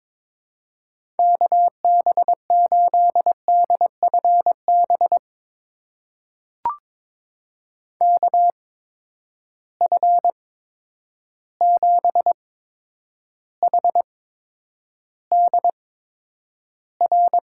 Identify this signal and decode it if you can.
Morse code, heard as a keyed tone, about 22 wpm, spaced as for 9 wpm, 700 Hz. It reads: KB8DFB KF7HDR
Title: USA Call Signs - 6 Characters - 8x Character Spacing - 22wpm